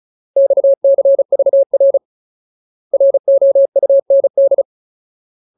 Morse code, heard as a keyed tone, about 34 wpm, 550 Hz. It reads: XCVR ROUND
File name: Ultimate Word List / 35WPM Ultimate